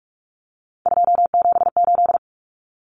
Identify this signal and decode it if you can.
Morse code, heard as a keyed tone, about 44 wpm, 700 Hz. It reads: 278
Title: Common Cut Numbers - 3-Digit 45wpm